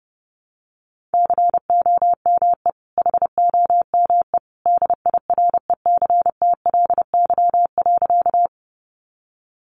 Morse code, heard as a keyed tone, about 28 wpm, 700 Hz. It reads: COME HOME DIRECTLY.
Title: Sentences from Top 800 Words 30wpm